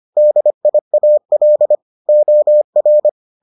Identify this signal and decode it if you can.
Morse code, heard as a keyed tone, about 25 wpm, 600 Hz. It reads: DIAL OR